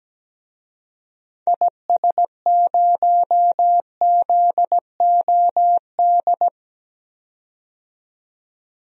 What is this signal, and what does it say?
Morse code, heard as a keyed tone, about 17 wpm, 700 Hz. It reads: IS0ZOD
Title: DXCC-100 Call Signs - 6 Characters - 17wpm